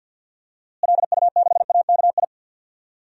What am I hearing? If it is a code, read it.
Morse code, heard as a keyed tone, about 53 wpm, 700 Hz. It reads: LU6AKI
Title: DXCC-100 Call Signs - 6 Characters - Repeated 3x - 50wpm